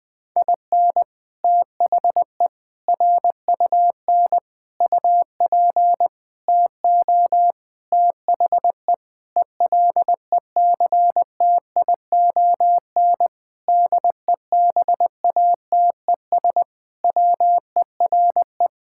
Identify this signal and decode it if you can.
Morse code, heard as a keyed tone, about 20 wpm, 700 Hz. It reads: IN THE RUN UP TO THE ELECTION DEBATES WERE
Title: Sentences with 6-Word Phrases - 20wpm